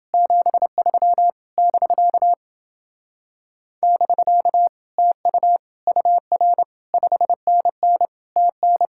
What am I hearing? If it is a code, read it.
Morse code, heard as a keyed tone, about 27 wpm, 700 Hz. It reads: 73 <BK> <BK> TU UR 5NN TN